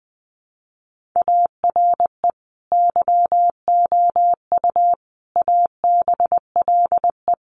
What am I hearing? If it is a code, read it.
Morse code, heard as a keyed tone, about 20 wpm, 700 Hz. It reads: ARE YOU ABLE
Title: Sentences from Top 1500 Words 20wpm